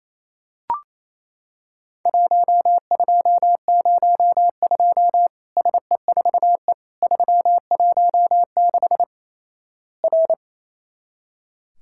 Morse code, heard as a keyed tone, about 28 wpm, 700 Hz. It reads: 1202 SE4E 316 R
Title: Worked All Europe DX Contest - QTC Exchange - 28wpm